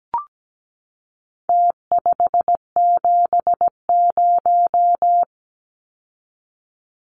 Morse code, heard as a keyed tone, about 17 wpm, 700 Hz. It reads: T570
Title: Numbers - Contest Style - 4-Digit 17wpm